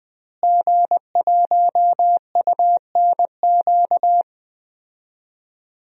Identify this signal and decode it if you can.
Morse code, heard as a keyed tone, about 20 wpm, 700 Hz. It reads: G1UNQ
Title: DXCC-100 Call Signs - 5 Characters - Repeated 3x - 20wpm